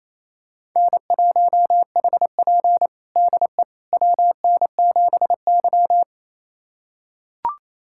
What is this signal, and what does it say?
Morse code, heard as a keyed tone, about 28 wpm, 690 Hz. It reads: N1HP DE WN7Y
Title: USA Callsign DE Callsign - 4 Characters - 28wpm